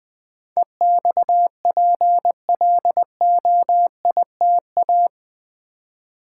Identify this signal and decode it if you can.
Morse code, heard as a keyed tone, about 20 wpm, 700 Hz. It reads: EXPLOITA
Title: Build A Word - 4 Letters At A Time - 12-Letter Words - 20wpm